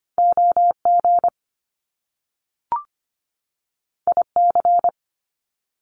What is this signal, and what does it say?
Morse code, heard as a keyed tone, about 25 wpm, 700 Hz. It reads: OG IC
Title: Top Two Letter Combinations in Words 25wpm